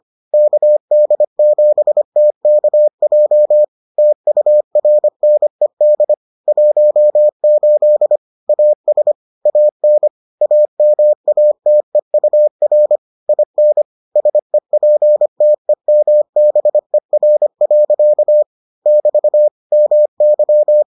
Morse code, heard as a keyed tone, about 25 wpm, 600 Hz. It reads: KD7TKJ TURNED 18 AS AN AMATEUR IN SEPTEMBER. <BT> MY